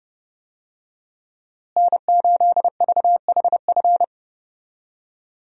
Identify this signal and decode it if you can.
Morse code, heard as a keyed tone, about 29 wpm, 700 Hz. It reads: N8VHF
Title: USA Call Signs - 5 Characters - Repeated 2x - 30wpm